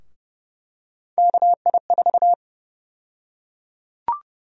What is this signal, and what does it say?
Morse code, heard as a keyed tone, about 30 wpm, 700 Hz. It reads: KI4
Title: USA Call Sign Prefixes - Repeated 2x - 30wpm